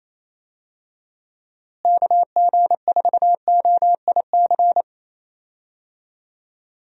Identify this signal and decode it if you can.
Morse code, heard as a keyed tone, about 28 wpm, 700 Hz. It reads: KG4OIC
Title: USA Call Signs - 6 Characters - Repeated 2x - 28wpm